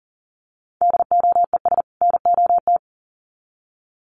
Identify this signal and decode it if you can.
Morse code, heard as a keyed tone, about 40 wpm, 700 Hz. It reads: DOES NOT
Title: Sets of 2 Words 40wpm